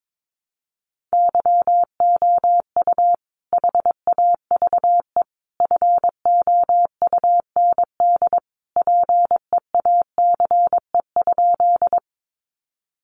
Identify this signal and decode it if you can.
Morse code, heard as a keyed tone, about 22 wpm, 700 Hz. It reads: YOU HAVE FOUND PEACE?
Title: Sentences from Top 1500 Words 22wpm